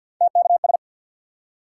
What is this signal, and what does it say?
Morse code, heard as a keyed tone, about 53 wpm, 700 Hz. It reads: TKS
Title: Top 90 QSO Elements 50wpm